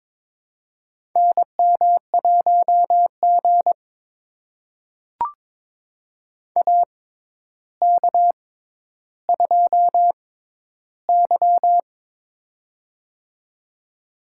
Morse code, heard as a keyed tone, about 22 wpm, spaced as for 11 wpm, 700 Hz. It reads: NM1G AK2Y
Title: USA Call Signs - 4 Characters - 6x Character Spacing - 22wpm